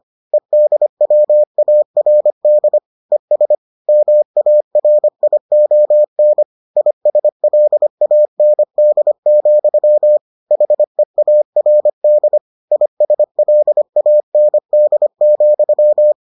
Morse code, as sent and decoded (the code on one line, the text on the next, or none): . -.. .-- .- .-. -.. / . ... / -- .- .-. .. --- -. / .. ... .-.. .- -. -.. --..-- / .... .- .-. -.. / .. ... .-.. .- -. -.. --..--
EDWARD ES MARION ISLAND, HEARD ISLAND,